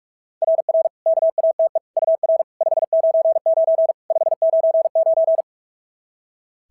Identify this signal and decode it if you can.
Morse code, heard as a keyed tone, about 46 wpm, 690 Hz. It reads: RR KATE UR 599 599
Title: SOTA QSO Practice 45wpm